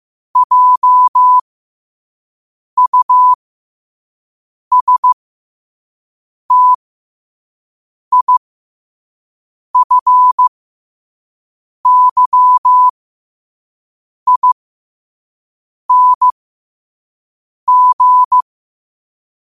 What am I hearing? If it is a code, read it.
Morse code, heard as a keyed tone, about 15 wpm, spaced as for 5 wpm, 1000 Hz. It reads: JUSTIFYING
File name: qotdfiles / 5wpm